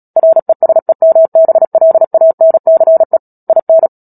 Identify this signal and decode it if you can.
Morse code, heard as a keyed tone, about 37 wpm, 650 Hz. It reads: RESEMBLANCE IN
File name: Book Two / 10-Chapter 10 - Epilogue - 36 WPM